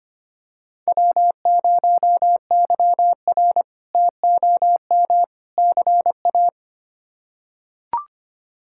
Morse code, heard as a keyed tone, about 25 wpm, 700 Hz. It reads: W0YR TOM CA
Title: K1USN SST QSO Practice - 25wpm